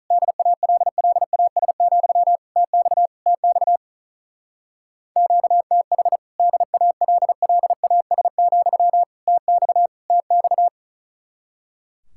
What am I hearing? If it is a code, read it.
Morse code, heard as a keyed tone, about 42 wpm, 700 Hz. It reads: DALLAS, TX TX QTH DALLAS, TX TX
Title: Speed-Racing Common QSO Phrases - 35wpm